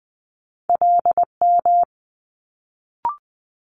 Morse code, heard as a keyed tone, about 20 wpm, 700 Hz. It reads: LM
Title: Multi-letter Phonemes Quick Fire 20wpm